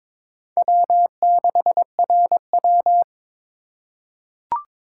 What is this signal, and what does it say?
Morse code, heard as a keyed tone, about 22 wpm, 700 Hz. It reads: W6RW
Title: USA Call Signs - 4 Characters - 2x Character Spacing - 22wpm